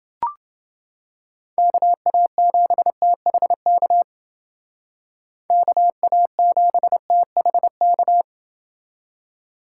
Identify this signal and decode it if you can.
Morse code, heard as a keyed tone, about 28 wpm, 700 Hz. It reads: KA7THK KA7THK